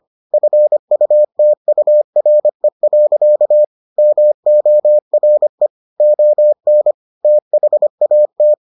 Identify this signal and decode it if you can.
Morse code, heard as a keyed tone, about 25 wpm, 600 Hz. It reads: FUTURE. MORE ON THAT